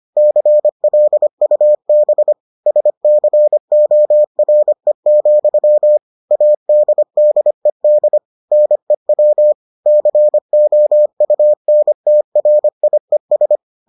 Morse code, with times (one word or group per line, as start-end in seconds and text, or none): 0.2-2.3 CLUB
2.7-6.0 SCORE,
6.3-8.2 ADDED
8.5-9.5 NEW
9.9-13.6 COUNTRIES